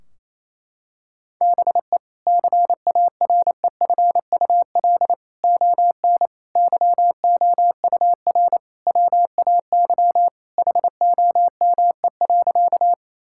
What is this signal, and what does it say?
Morse code, heard as a keyed tone, about 28 wpm, 700 Hz. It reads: BE CAREFUL ON YOUR WAY HOME.
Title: Sentences from Top 1600 Words 28wpm